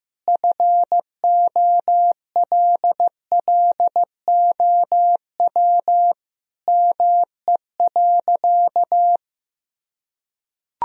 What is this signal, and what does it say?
Morse code, heard as a keyed tone, about 15 wpm, 700 Hz. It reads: FOLLOW ME.
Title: Sentences from Top 1000 Words - Review Part 1 - 15wpm